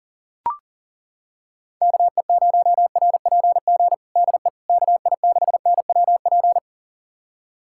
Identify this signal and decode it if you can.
Morse code, heard as a keyed tone, about 37 wpm, 690 Hz. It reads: KE0RPG DE KI6NWP
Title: USA Callsign DE Callsign - 6 Characters - 40wpm